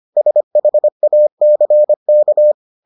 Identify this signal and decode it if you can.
Morse code, heard as a keyed tone, about 25 wpm, 600 Hz. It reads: SHACK